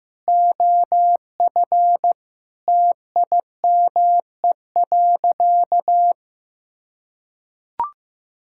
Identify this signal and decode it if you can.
Morse code, heard as a keyed tone, about 15 wpm, 700 Hz. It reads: OF TIME.
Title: Sentences from Top 1000 Words - Review Part 5 - 15wpm